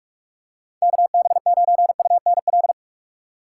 Morse code, heard as a keyed tone, about 46 wpm, 700 Hz. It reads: KB9UNL
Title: USA Call Signs - 6 Characters - 3x Character Spacing - 45wpm